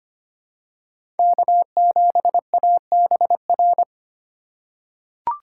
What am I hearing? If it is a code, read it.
Morse code, heard as a keyed tone, about 25 wpm, 700 Hz. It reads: K7ABR
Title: USA Call Signs - 5 Characters - 4x Character Spacing - 25wpm